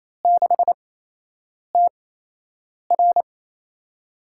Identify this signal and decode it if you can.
Morse code, heard as a keyed tone, about 28 wpm, spaced as for 8 wpm, 700 Hz. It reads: 6TR
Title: USA Call Signs - 4 Characters - 8x Character Spacing - 28wpm